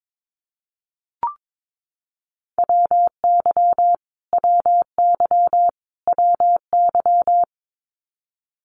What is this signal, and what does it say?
Morse code, heard as a keyed tone, about 22 wpm, 700 Hz. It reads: WY WY WY
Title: US States Repeated 3x 22wpm